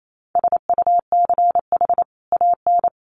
Morse code, heard as a keyed tone, about 28 wpm, 700 Hz. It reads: SUCH AN
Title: Sentences from Top 1100 Words 28wpm